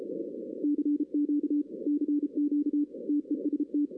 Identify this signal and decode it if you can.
Morse code, heard as a keyed tone, about 34 wpm, 290 Hz. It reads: CQ CQ TEST
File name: 2016 arrl dx cw / 2016-ARRL-DX-CW-K5TR- - 0-9